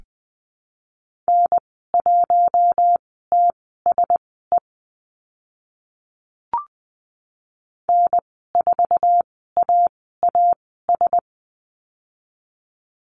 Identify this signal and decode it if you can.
Morse code, heard as a keyed tone, about 20 wpm, spaced as for 15 wpm, 700 Hz. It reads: N1TSE N4AAS
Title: 055 Lesson 10g - Letter r - Call Signs 20wpm